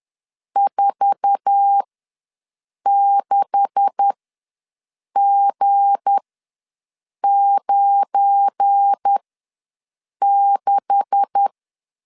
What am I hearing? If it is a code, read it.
Morse code, heard as a keyed tone, about 10 wpm, spaced as for 6 wpm, 800 Hz. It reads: 46G96
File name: LEZIONI / 9 286 ws 10wpm 4032